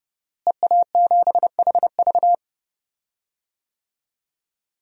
Morse code, heard as a keyed tone, about 30 wpm, 700 Hz. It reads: EA7HV
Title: DXCC-100 Call Signs - 5 Characters - Repeated 2x - 30wpm